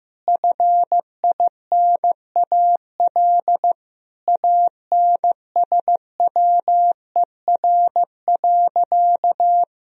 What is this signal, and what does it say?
Morse code, heard as a keyed tone, about 15 wpm, 700 Hz. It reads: FINAL ANSWER.
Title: Sentences from Top 1000 Words - Review Part 2 - 15wpm